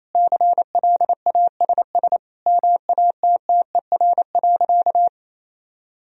Morse code, heard as a keyed tone, about 28 wpm, 700 Hz. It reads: CLASS MATTER.
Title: Sentences from Top 1000 Words - Review Part 2 - 28wpm